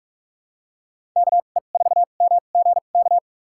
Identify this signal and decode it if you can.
Morse code, heard as a keyed tone, about 46 wpm, spaced as for 33 wpm, 700 Hz. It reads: KE4MGK
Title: USA Call Signs - 6 Characters - 2x Character Spacing - 45wpm